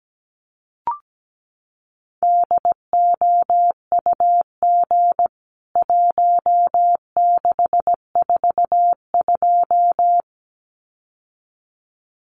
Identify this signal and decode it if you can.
Morse code, heard as a keyed tone, about 17 wpm, 700 Hz. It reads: DOUG 1642
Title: CWT Exchange 17wpm